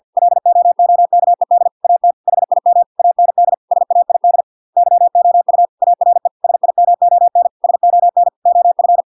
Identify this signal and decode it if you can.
Morse code, heard as a keyed tone, about 50 wpm, 700 Hz. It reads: LOOKED AT HIM AND SAID YOU ARE SIMON SON OF